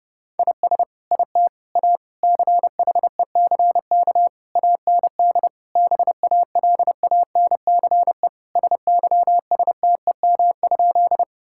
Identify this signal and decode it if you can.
Morse code, heard as a keyed tone, about 30 wpm, 700 Hz. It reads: IS IT A CHECK AND BALANCE SYSTEM?
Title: Sentences from Top 1000 Words - Review Part 6 - 30wpm